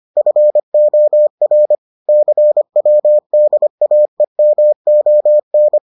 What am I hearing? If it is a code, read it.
Morse code, heard as a keyed tone, about 25 wpm, 600 Hz. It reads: FOR CWDAEMON